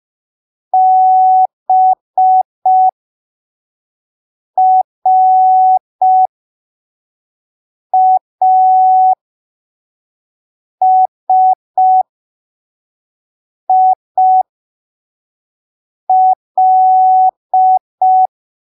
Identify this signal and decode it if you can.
Morse code, heard as a keyed tone, about 5 wpm, spaced as for 3 wpm, 750 Hz. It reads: BRASIL
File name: arquivos / texto / texto 14 05ppm